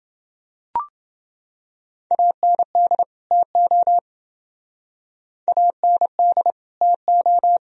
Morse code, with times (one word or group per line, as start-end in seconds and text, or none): 2.1-3.0 AND
3.3-4.0 TO
5.5-6.5 AND
6.8-7.6 TO